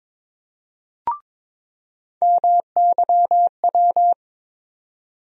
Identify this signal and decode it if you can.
Morse code, heard as a keyed tone, about 22 wpm, 700 Hz. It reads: MYW
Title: USA Call Sign Postfixes - 3 Letters 22wpm